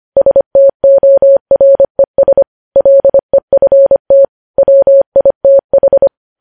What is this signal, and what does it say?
Morse code, heard as a keyed tone, about 25 wpm, 550 Hz. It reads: STORES LEFT WITH
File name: podcast / News headlines-1x-25x25